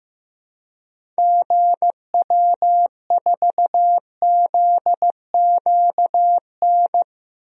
Morse code, heard as a keyed tone, about 15 wpm, 700 Hz. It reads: GW4ZQN
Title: DXCC-100 Call Signs - 6 Characters - Repeated 3x - 15wpm